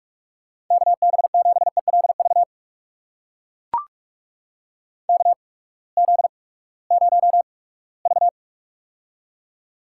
Morse code, heard as a keyed tone, about 44 wpm, spaced as for 22 wpm, 700 Hz. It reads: KB7ELV KZ0V
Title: USA Call Signs - 8x Character Spacing - 45wpm